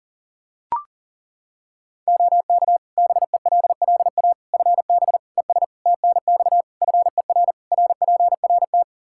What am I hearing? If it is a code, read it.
Morse code, heard as a keyed tone, about 40 wpm, 700 Hz. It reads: OK BELLA FB ES TNX FER RPRT
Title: K1USN SST QSO Practice 40wpm